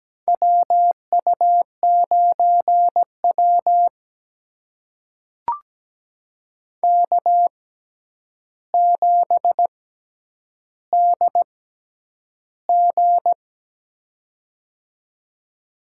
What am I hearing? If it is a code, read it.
Morse code, heard as a keyed tone, about 17 wpm, spaced as for 9 wpm, 700 Hz. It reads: WU9W K7DG
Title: USA Call Signs - 6x Character Spacing - 17wpm